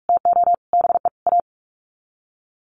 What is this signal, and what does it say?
Morse code, heard as a keyed tone, about 44 wpm, 700 Hz. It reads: TO BE A
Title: Sets of 5 Words Encore2 45wpm